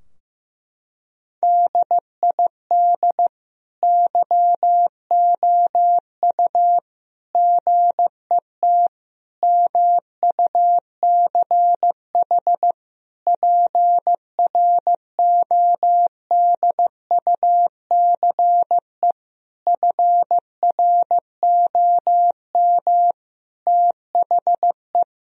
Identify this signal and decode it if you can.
Morse code, heard as a keyed tone, about 15 wpm, 700 Hz. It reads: DID YOU GET MUCH PRODUCE FROM THE
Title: Sentences from Top 1000 Words - Review Part 3 - 15wpm